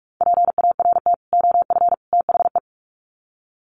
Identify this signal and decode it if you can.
Morse code, heard as a keyed tone, about 44 wpm, 700 Hz. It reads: PART OF THE
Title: Sets of 3 Words 45wpm